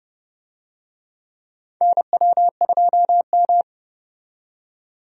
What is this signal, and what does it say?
Morse code, heard as a keyed tone, about 32 wpm, 700 Hz. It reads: NW2M
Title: USA Call Signs - 4 Characters - 30wpm